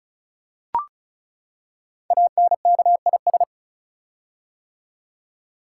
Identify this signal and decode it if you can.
Morse code, heard as a keyed tone, about 36 wpm, 700 Hz. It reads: ANKIS
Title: Code Groups - 5 Letters - 35wpm